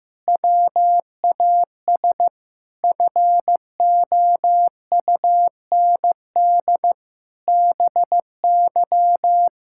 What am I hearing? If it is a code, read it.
Morse code, heard as a keyed tone, about 15 wpm, 700 Hz. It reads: WAS FOUND BY
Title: Sentences from Top 1600 Words 15wpm